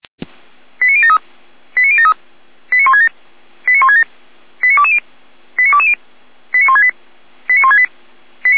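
Five-tone selective call (ZVEI), heard as quick runs of five tones, tones of about 70 ms. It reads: 89073 89073 89167 89167 89220 89220 89168 89168
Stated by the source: ZVEI five-tone call 89022 received by SDR.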